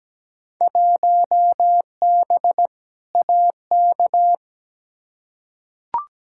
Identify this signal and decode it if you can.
Morse code, heard as a keyed tone, about 17 wpm, 700 Hz. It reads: 1B AK
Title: Field Day Exchanges Encore - 17wpm